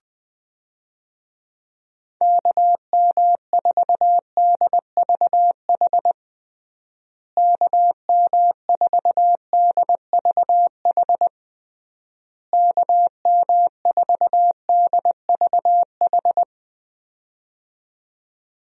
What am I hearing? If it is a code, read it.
Morse code, heard as a keyed tone, about 21 wpm, 700 Hz. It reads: KM4DVH KM4DVH KM4DVH
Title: USA Call Signs - 6 Characters - Repeated 3x - 20wpm